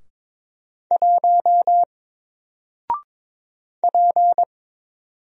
Morse code, heard as a keyed tone, about 22 wpm, 700 Hz. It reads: 1 P
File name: Single-Letter-Number-22wpm